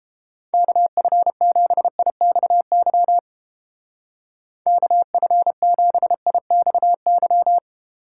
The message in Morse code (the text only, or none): KF7IXY KF7IXY